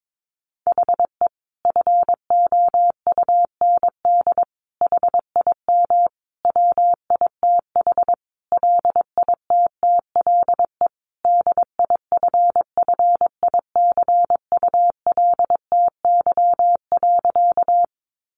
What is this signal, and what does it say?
Morse code, heard as a keyed tone, about 22 wpm, 700 Hz. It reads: HE FOUND HIM WITH LITTLE DIFFICULTY.
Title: Sentences from Top 1500 Words 22wpm